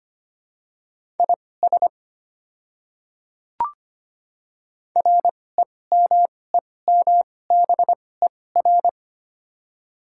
Morse code, heard as a keyed tone, about 25 wpm, spaced as for 18 wpm, 700 Hz. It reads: IS REMEMBER
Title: 2x Character Spacing - Top 500 Words - 25wpm